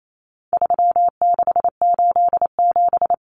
Morse code, heard as a keyed tone, about 28 wpm, 700 Hz. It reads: 3687
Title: Numbers - 4-Digit 28wpm